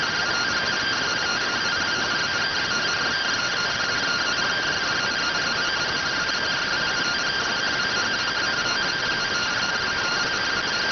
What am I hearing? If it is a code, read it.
Two-tone teleprinter signal, about 170 Hz shift.